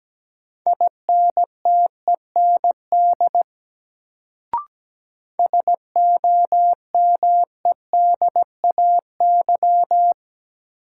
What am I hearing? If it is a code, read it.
Morse code, heard as a keyed tone, about 17 wpm, 700 Hz. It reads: INTEND SOMEDAY
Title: Sentences from Top 1600 Words - New Word Drill - 17wpm